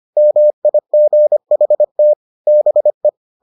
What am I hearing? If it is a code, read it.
Morse code, heard as a keyed tone, about 25 wpm, 600 Hz. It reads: MIGHT BE